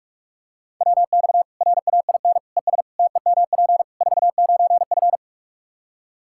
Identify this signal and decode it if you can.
Morse code, heard as a keyed tone, about 42 wpm, 700 Hz. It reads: WX RAIN ES TEMP 49F
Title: Practice QSO - Exchange 2 - Info - 45wpm